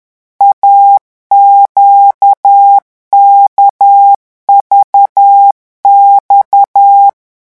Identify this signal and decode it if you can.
Morse code, heard as a keyed tone, about 10 wpm, 800 Hz. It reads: AQKVX
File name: morse mp3 / revision2-vitesse 10 mots